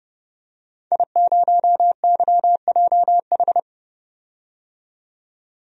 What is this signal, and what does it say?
Morse code, heard as a keyed tone, about 30 wpm, 700 Hz. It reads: I0YJH
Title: DXCC-100 Call Signs - 5 Characters - 30wpm